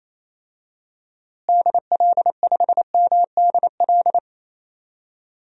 Morse code, heard as a keyed tone, about 28 wpm, 700 Hz. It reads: DL5MDL
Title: DXCC-100 Call Signs - 6 Characters - Repeated 2x - 28wpm